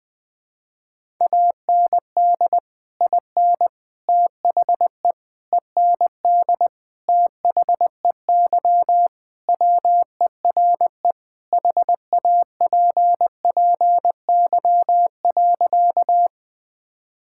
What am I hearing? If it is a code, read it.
Morse code, heard as a keyed tone, about 20 wpm, 700 Hz. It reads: AND IN THE END THEY WERE HAPPY.
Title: Sentences with 2-Word Phrases - Spoken Beginning - 20wpm